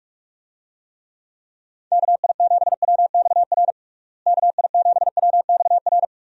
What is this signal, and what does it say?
Morse code, heard as a keyed tone, about 46 wpm, 700 Hz. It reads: KI7WXR KI7WXR
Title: USA Call Signs - 6 Characters - Repeated 3x - 45wpm